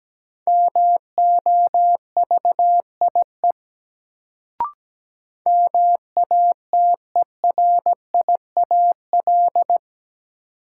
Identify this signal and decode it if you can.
Morse code, heard as a keyed tone, about 17 wpm, 700 Hz. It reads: MOVIE MATERIAL
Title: Sentences from Top 1000 Words - Review Part 5 - New Word Drill - 17wpm